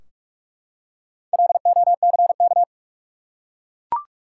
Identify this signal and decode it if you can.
Morse code, heard as a keyed tone, about 44 wpm, 700 Hz. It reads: LOCK
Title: Four Letter Words - 45wpm